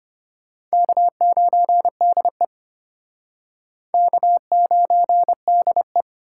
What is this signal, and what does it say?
Morse code, heard as a keyed tone, about 34 wpm, 700 Hz. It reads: K9DE K9DE